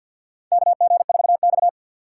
Morse code, heard as a keyed tone, about 53 wpm, 700 Hz. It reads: KG4X
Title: USA Call Signs - 4 Characters - 6x Character Spacing - 50wpm